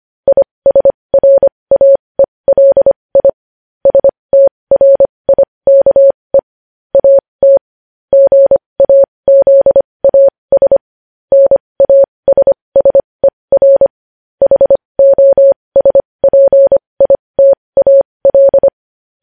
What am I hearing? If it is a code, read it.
Morse code, heard as a keyed tone, about 25 wpm, spaced as for 20 wpm, 550 Hz. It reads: ISRAELI STRIKE AT GAZAS NASSER HOSPITAL
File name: podcast / News headlines-1x-25x20